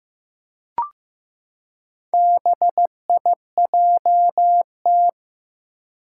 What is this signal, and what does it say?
Morse code, heard as a keyed tone, about 15 wpm, 700 Hz. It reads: BIJT
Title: Four Challenging Letter-Number Combinations 15wpm